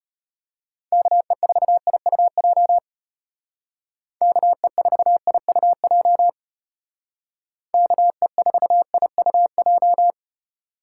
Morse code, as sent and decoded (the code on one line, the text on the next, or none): -.- . ....- .. ..- .--- / -.- . ....- .. ..- .--- / -.- . ....- .. ..- .---
KE4IUJ KE4IUJ KE4IUJ